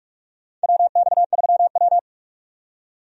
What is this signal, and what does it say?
Morse code, heard as a keyed tone, about 41 wpm, 700 Hz. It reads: WX3W
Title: USA Call Signs - 4 Characters - 45wpm